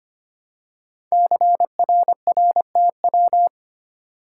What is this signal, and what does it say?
Morse code, heard as a keyed tone, about 25 wpm, 690 Hz. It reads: CRRTW
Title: Code Groups - 5 Letters - 25wpm